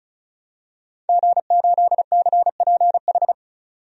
Morse code, heard as a keyed tone, about 34 wpm, 700 Hz. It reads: G8CPH
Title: DXCC-100 Call Signs - 5 Characters - 35wpm